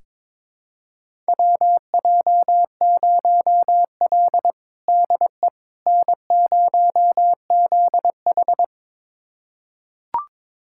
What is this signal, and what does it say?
Morse code, heard as a keyed tone, about 22 wpm, 700 Hz. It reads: WJ0L DE N0ZH
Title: USA Callsign DE Callsign - 4 Characters - 22wpm